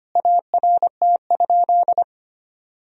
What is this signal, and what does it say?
Morse code, heard as a keyed tone, about 25 wpm, 700 Hz. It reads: ART?
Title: Sentences from Top 1000 Words - Review Part 3 - 25wpm